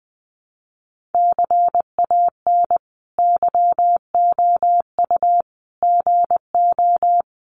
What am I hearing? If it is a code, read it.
Morse code, heard as a keyed tone, about 20 wpm, 700 Hz. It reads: CAN YOU GO
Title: Sentences from Top 1500 Words 20wpm